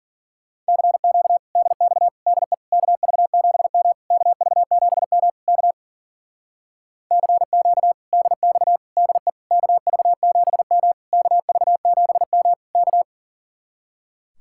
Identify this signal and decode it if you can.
Morse code, heard as a keyed tone, about 52 wpm, 700 Hz. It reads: CQ DX DE KV7M KV7M K CQ DX DE KV7M KV7M K